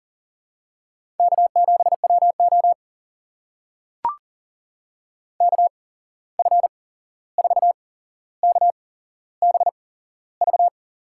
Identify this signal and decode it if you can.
Morse code, heard as a keyed tone, about 41 wpm, spaced as for 15 wpm, 700 Hz. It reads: K7WO KF4KBV
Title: USA Call Signs - 8x Character Spacing - 40wpm